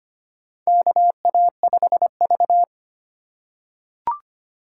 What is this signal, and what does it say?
Morse code, heard as a keyed tone, about 25 wpm, 700 Hz. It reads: KA5V
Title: USA Call Signs - 3x Character Spacing - 25wpm